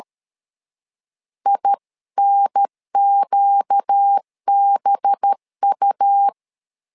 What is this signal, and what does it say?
Morse code, heard as a keyed tone, about 13 wpm, 800 Hz. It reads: INQBU
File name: LEZIONI / 5 LBVC / 5 LBVC 12wpm 1955